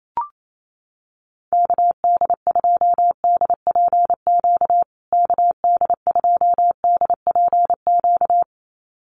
Morse code, heard as a keyed tone, about 28 wpm, 700 Hz. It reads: KD2DPQ KD2DPQ
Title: USA Call Signs Repeated 2x 28wpm